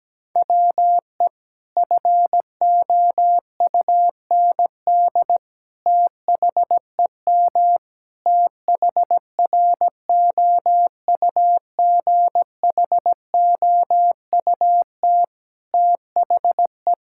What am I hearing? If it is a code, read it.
Morse code, heard as a keyed tone, about 17 wpm, 700 Hz. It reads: WE FOUND THEM THROUGHOUT THE
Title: Sentences from Top 1000 Words - Review Part 5 - 17wpm